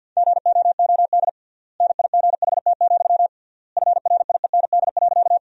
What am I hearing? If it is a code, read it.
Morse code, heard as a keyed tone, about 50 wpm, 700 Hz. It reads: GOOD NIGHT, FRIEND.
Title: 3x Word-Spacing - Sentences from Top 300 Words - 50wpm